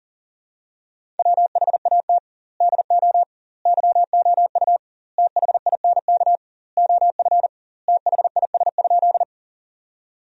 Morse code, heard as a keyed tone, about 40 wpm, 700 Hz. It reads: WHAT DO YOU THINK OF THIS?
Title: 2x Word-Spacing - Sentences from Top 200 Words - 40wpm